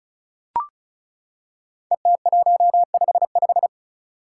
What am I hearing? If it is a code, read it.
Morse code, heard as a keyed tone, about 36 wpm, 700 Hz. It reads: ET155
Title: Five Challenging Letter-Number Combinations 35wpm